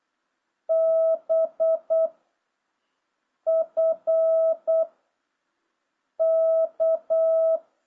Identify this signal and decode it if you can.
Morse code, heard as a keyed tone, about 8 wpm, 650 Hz. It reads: B F K